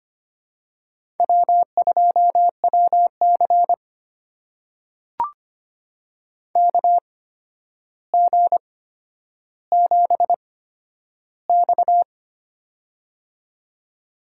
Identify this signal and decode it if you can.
Morse code, heard as a keyed tone, about 25 wpm, spaced as for 11 wpm, 700 Hz. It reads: W2WC KG7X
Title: USA Call Signs - 4 Characters - 8x Character Spacing - 25wpm